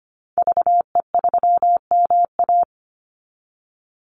Morse code, heard as a keyed tone, about 25 wpm, 700 Hz. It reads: VE3MA
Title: Call Signs Repeated 3x - 25wpm